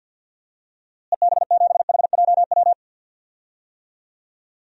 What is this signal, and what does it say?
Morse code, heard as a keyed tone, about 53 wpm, 700 Hz. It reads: EB7HJW
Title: DXCC-100 Call Signs - 6 Characters - Repeated 2x - 50wpm